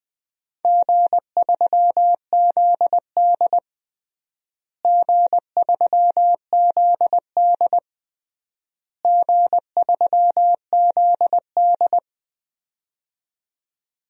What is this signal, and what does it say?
Morse code, heard as a keyed tone, about 20 wpm, 700 Hz. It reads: G3ZD G3ZD G3ZD
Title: DXCC-100 Call Signs - 4 Characters - Repeated 3x - 20wpm